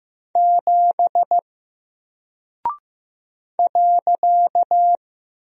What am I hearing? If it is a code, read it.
Morse code, heard as a keyed tone, about 15 wpm, 700 Hz. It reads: 7 .
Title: Single Character (Letter:Number:Punctuation:Prosign) Alt 15wpm